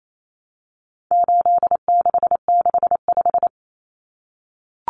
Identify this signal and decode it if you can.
Morse code, heard as a keyed tone, about 28 wpm, 700 Hz. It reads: 8665
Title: Common Cut Numbers - 4-Digit 28wpm